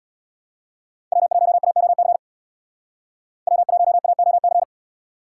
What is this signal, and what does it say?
Morse code, heard as a keyed tone, about 61 wpm, 700 Hz. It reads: W1AOC W1AOC
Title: Speed Racing - USA 1x3 Call Signs 50wpm